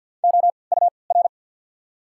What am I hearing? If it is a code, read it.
Morse code, heard as a keyed tone, about 54 wpm, spaced as for 29 wpm, 690 Hz. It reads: OUR
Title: 3x Character Spacing - Top 500 Words - 50wpm